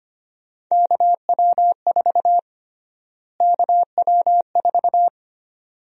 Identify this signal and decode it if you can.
Morse code, heard as a keyed tone, about 25 wpm, 700 Hz. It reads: KW4 KW4
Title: USA Call Sign Prefixes - Repeated 2x - 25wpm